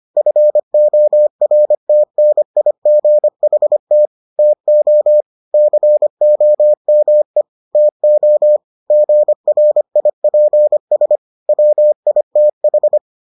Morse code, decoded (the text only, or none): FORTNIGHT TO COME TO GRIPS WITH